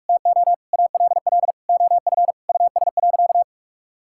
Morse code, heard as a keyed tone, about 47 wpm, 700 Hz. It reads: TO ALL OF US.
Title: Sentences from Top 1000 Words - Review Part 5 - 45wpm